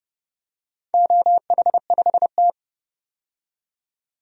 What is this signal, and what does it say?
Morse code, heard as a keyed tone, about 30 wpm, 700 Hz. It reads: OH5T
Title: DXCC-100 Call Signs - 4 Characters - 30wpm